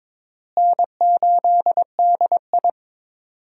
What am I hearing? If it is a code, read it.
Morse code, heard as a keyed tone, about 22 wpm, 700 Hz. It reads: N8DI
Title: USA Call Signs - 4 Characters - Repeated 3x - 22wpm